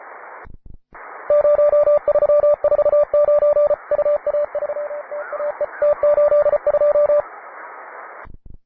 Morse code, heard as a keyed tone, about 34 wpm, 590 Hz, with radio noise. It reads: 0349 UA3TAE T82
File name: wae / cw advanced / py2na cw